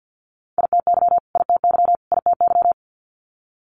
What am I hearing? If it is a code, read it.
Morse code, heard as a keyed tone, about 50 wpm, 690 Hz. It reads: ITY ITY ITY
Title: Multi-letter Phonemes Repeated 3x 50wpm